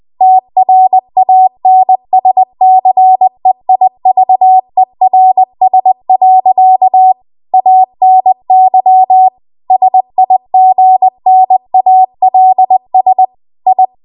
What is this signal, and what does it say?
Morse code, heard as a keyed tone, about 20 wpm, 750 Hz. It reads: TRANSCEIVERS. ANY SIGNALS I
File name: Archive / 20 WPM / 20WPM